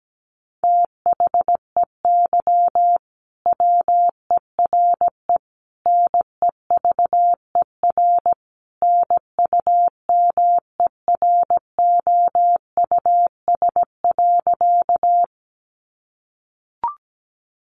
Morse code, heard as a keyed tone, about 17 wpm, 700 Hz. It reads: THEY WERE NEVER NUMEROUS.
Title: Sentences from Top 1400 Words 17wpm